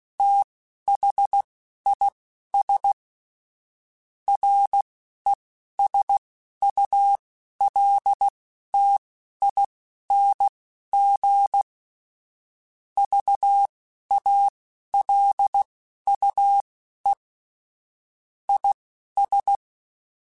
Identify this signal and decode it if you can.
Morse code, heard as a keyed tone, about 16 wpm, spaced as for 11 wpm, 800 Hz. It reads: THIS RESULTING VALUE IS
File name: Technical Subjects / Technical Subjects-mp3s / XC @10